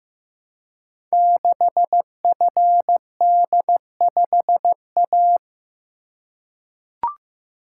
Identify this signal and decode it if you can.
Morse code, heard as a keyed tone, about 15 wpm, 700 Hz. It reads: 6FD5A
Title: Code Groups - 5 Letter:Numbers - 15wpm